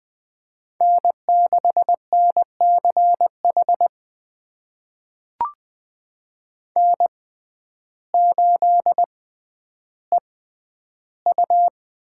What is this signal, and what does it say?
Morse code, heard as a keyed tone, about 20 wpm, spaced as for 11 wpm, 700 Hz. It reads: N6NCH N8EU
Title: USA Call Signs - 5 Characters - 6x Character Spacing - 20wpm